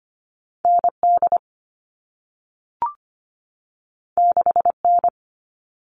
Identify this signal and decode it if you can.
Morse code, heard as a keyed tone, about 25 wpm, 700 Hz. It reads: ND 6N
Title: All Cut Numbers - 2-Digit 25wpm